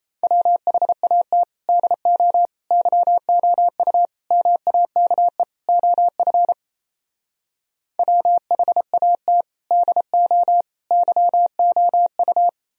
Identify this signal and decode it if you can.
Morse code, heard as a keyed tone, about 36 wpm, 700 Hz. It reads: WHAT DO YOU MAKE OF WHAT DO YOU